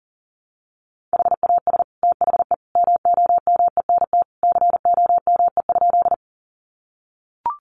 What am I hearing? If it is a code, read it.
Morse code, heard as a keyed tone, about 40 wpm, 700 Hz. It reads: HAS THE MOMENT COME?
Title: Sentences from Top 1000 Words 40wpm